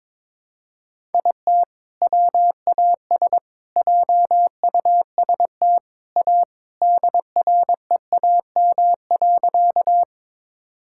Morse code, heard as a keyed tone, about 21 wpm, 700 Hz. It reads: IT WAS JUST A DREAM.
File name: practice / 22wpm / Sentences with 3-Word Phrases - 22wpm